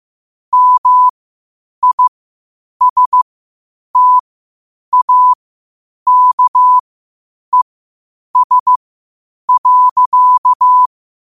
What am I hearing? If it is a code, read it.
Morse code, heard as a keyed tone, about 15 wpm, spaced as for 8 wpm, 1000 Hz. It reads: MISTAKES.
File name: qotdfiles / 8wpm